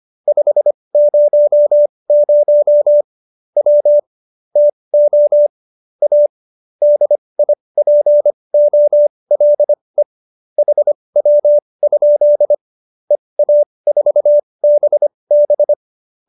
Morse code, heard as a keyed tone, about 25 wpm, spaced as for 20 wpm, 590 Hz. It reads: 500 W TO A DIPOLE HW? EA4BB
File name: qso215b 20